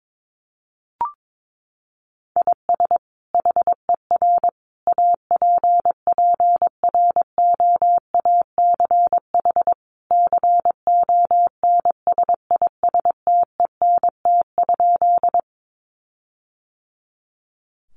Morse code, heard as a keyed tone, about 22 wpm, 700 Hz. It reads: IS HER APPROACH CONSISTENT?
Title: Sentences from Top 1500 Words 22wpm